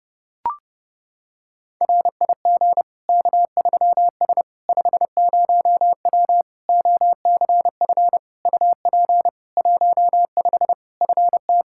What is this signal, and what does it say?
Morse code, heard as a keyed tone, about 30 wpm, 700 Hz. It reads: RIG K3S 50W OCF UP 15 FT
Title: K1USN SST QSO Practice 30wpm